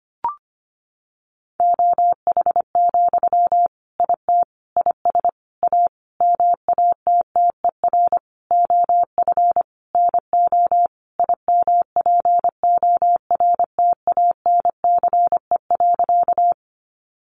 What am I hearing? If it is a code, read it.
Morse code, heard as a keyed tone, about 25 wpm, 700 Hz. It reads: OH, IT IS A MATTER OF NO IMPORTANCE.
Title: Sentences from Top 1200 Words 25wpm